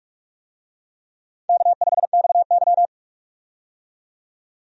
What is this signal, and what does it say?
Morse code, heard as a keyed tone, about 44 wpm, 700 Hz. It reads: K5XY
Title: USA Call Signs - 4 Characters - 45wpm